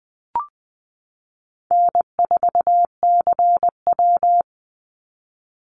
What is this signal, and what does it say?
Morse code, heard as a keyed tone, about 20 wpm, 700 Hz. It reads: N4CW
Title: Call Signs Repeated 2x - 20wpm